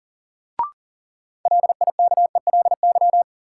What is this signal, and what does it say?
Morse code, heard as a keyed tone, about 40 wpm, 700 Hz. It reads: LIKELY
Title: Sentences from Top 600 Words - New Word Drill - 40wpm